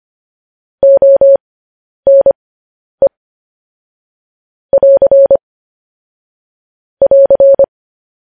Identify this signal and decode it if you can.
Morse code, heard as a keyed tone, about 25 wpm, spaced as for 10 wpm, 550 Hz. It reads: ONE <AR> <AR>
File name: podcast / News headlines-1x-25x10